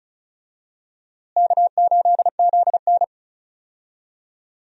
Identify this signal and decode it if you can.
Morse code, heard as a keyed tone, about 34 wpm, 700 Hz. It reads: K8ZN
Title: USA Call Signs - 4 Characters - Repeated 3x - 35wpm